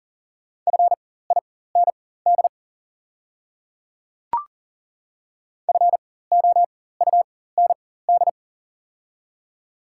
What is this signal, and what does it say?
Morse code, heard as a keyed tone, about 40 wpm, spaced as for 19 wpm, 700 Hz. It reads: FIND FOUND
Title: 4x Character Spacing - Top 300 Words - 40wpm